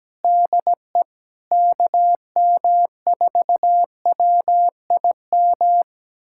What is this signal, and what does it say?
Morse code, heard as a keyed tone, about 17 wpm, 700 Hz. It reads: DE KM4WIM
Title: USA Callsign DE Callsign - 6 Characters - 17wpm